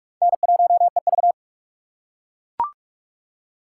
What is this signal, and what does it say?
Morse code, heard as a keyed tone, about 40 wpm, 700 Hz. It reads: N1EV